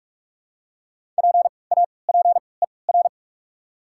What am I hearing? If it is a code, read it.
Morse code, heard as a keyed tone, about 46 wpm, 690 Hz. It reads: P A P E R
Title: 3x Character Spacing - Top 500 Words - 45wpm